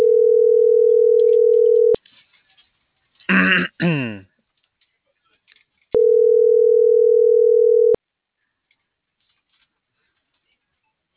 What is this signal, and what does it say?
Ringback tone (North America), 440 + 480 Hz.